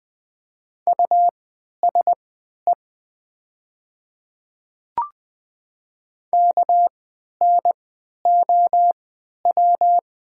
Morse code, heard as a keyed tone, about 20 wpm, spaced as for 11 wpm, 700 Hz. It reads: USE KNOW
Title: 3x Character Spacing - Top 200 Words - 20wpm